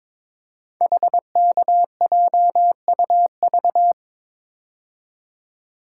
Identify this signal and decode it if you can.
Morse code, heard as a keyed tone, about 22 wpm, 700 Hz. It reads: HKJUV
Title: Code Groups - 5 Letters - 22wpm